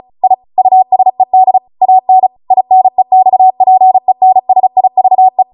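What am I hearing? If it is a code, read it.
Morse code, heard as a keyed tone, about 36 wpm, 750 Hz. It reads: I USED AN INEXPENSIVE
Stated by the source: Amateur radio Morse code practice archive transmission at 35 wpm.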